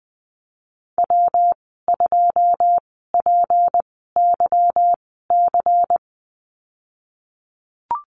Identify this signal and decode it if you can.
Morse code, heard as a keyed tone, about 20 wpm, spaced as for 15 wpm, 700 Hz. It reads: W2PYC
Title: 154 Lesson 25g - Letter p - Call Signs 20wpm